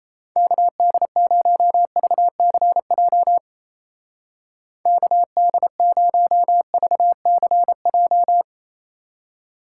Morse code, heard as a keyed tone, about 35 wpm, 700 Hz. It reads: KD0VCJ KD0VCJ
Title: Speed Racing - USA 2x3 Call Signs 28wpm